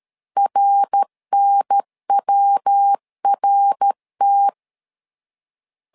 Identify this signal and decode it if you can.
Morse code, heard as a keyed tone, about 13 wpm, 800 Hz. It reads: RNWRT
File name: LEZIONI / 4 Q1 12wpm 2028